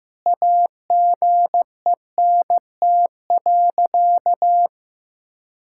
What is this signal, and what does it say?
Morse code, heard as a keyed tone, about 15 wpm, 700 Hz. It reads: AGENT.
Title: Sentences from Top 1600 Words 15wpm